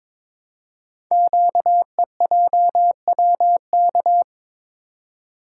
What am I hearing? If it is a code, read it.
Morse code, heard as a keyed tone, about 22 wpm, 700 Hz. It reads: QEJWK
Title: Code Groups - 5 Letters - 22wpm